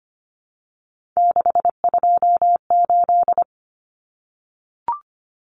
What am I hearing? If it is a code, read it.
Morse code, heard as a keyed tone, about 25 wpm, 700 Hz. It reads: 628
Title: Numbers - 3-Digit 25wpm